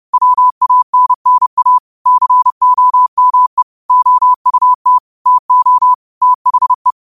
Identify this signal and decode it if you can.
Morse code, heard as a keyed tone, about 30 wpm, 1000 Hz. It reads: WANNA COME OUT TO THE
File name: qotdfiles / 30wpm